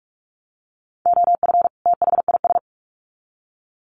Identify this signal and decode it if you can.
Morse code, heard as a keyed tone, about 44 wpm, 700 Hz. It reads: OF THIS
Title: Sets of 2 Words 45wpm